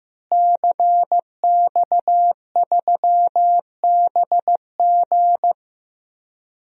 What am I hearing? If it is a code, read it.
Morse code, heard as a keyed tone, about 15 wpm, 700 Hz. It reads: CX3BG
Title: DXCC-100 Call Signs - 5 Characters - Repeated 2x - 15wpm